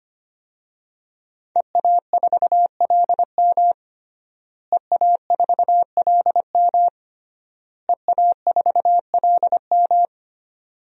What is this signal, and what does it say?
Morse code, heard as a keyed tone, about 25 wpm, 700 Hz. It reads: EA4LM EA4LM EA4LM
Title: DXCC-100 Call Signs - 5 Characters - Repeated 3x - 25wpm